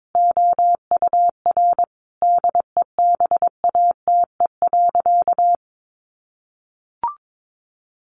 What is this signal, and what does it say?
Morse code, heard as a keyed tone, about 22 wpm, 700 Hz. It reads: OUR DEBATE.
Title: Sentences from Top 1100 Words 22wpm